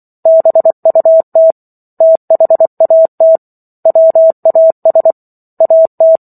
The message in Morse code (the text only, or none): BUT THAT WAS AT